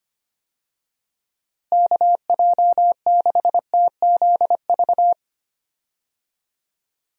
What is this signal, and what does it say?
Morse code, heard as a keyed tone, about 25 wpm, 700 Hz. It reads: KJ6TZV
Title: USA Call Signs - 6 Characters - Repeated 2x - 25wpm